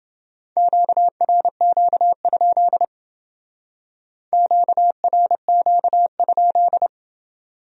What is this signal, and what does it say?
Morse code, heard as a keyed tone, about 28 wpm, 700 Hz. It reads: QRQ? QRQ?